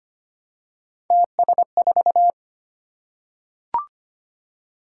Morse code, heard as a keyed tone, about 25 wpm, 700 Hz. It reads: TS4
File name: Three Letter-Number Combinations - 25wpm